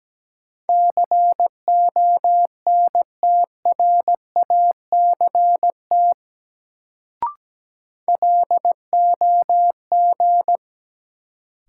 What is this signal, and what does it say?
Morse code, heard as a keyed tone, about 17 wpm, 700 Hz. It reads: CONTRACT LOG
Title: Sentences from Top 1400 Words - New Word Drill - 17wpm